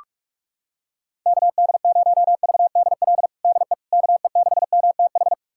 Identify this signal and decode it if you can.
Morse code, heard as a keyed tone, about 46 wpm, 700 Hz. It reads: KD0VDL DE KE6MTH
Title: USA Callsign DE Callsign - 6 Characters - 45wpm